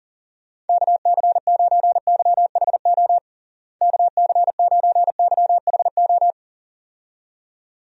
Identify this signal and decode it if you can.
Morse code, heard as a keyed tone, about 40 wpm, 700 Hz. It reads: KC9YHO KC9YHO
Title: USA Call Signs - 6 Characters - Repeated 2x - 40wpm